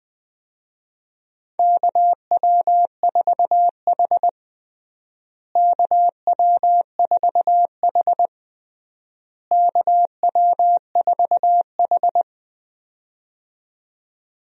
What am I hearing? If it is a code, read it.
Morse code, heard as a keyed tone, about 20 wpm, 700 Hz. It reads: KW4H KW4H KW4H
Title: USA Call Signs - 4 Characters - Repeated 3x - 20wpm